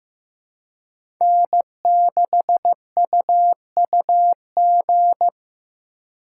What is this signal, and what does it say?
Morse code, heard as a keyed tone, about 18 wpm, 700 Hz. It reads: N6UUG
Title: Speed Racing - USA Call Signs 15wpm